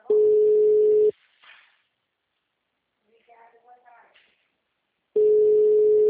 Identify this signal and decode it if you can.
Ringback tone (Europe, 425 Hz), with speech.